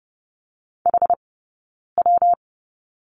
Morse code, heard as a keyed tone, about 30 wpm, 700 Hz. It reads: H W
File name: Lesson 19d - Letter w - Two Characters 30wpm